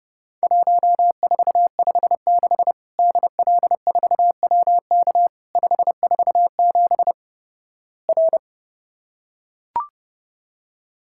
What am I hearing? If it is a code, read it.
Morse code, heard as a keyed tone, about 31 wpm, 700 Hz. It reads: 1456 DL4WK 547 R
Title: Worked All Europe DX Contest - QTC Exchange - 30wpm